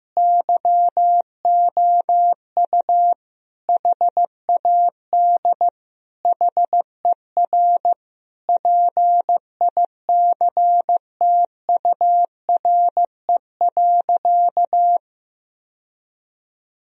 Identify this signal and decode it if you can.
Morse code, heard as a keyed tone, about 14 wpm, 700 Hz. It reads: YOU HAD HER PICTURE.
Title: Sentences from Top 1000 Words - Review Part 1 - 15wpm